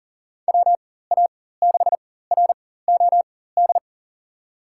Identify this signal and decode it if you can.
Morse code, heard as a keyed tone, about 40 wpm, spaced as for 19 wpm, 700 Hz. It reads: WA6ROD
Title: USA Call Signs - 4x Character Spacing - 40wpm